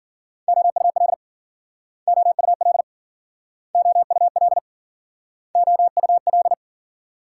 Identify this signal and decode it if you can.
Morse code, heard as a keyed tone, about 51 wpm, 700 Hz. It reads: OUL OUL OUL OUL